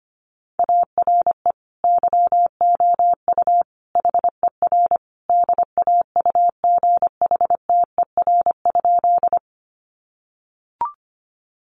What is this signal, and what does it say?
Morse code, heard as a keyed tone, about 25 wpm, 700 Hz. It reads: ARE YOU HER DAUGHTER?
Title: Sentences from Top 1500 Words 25wpm